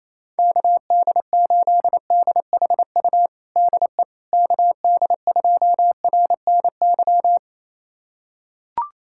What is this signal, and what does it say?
Morse code, heard as a keyed tone, about 28 wpm, 700 Hz. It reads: KD8DHU DE KD2RNY
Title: USA Callsign DE Callsign 28wpm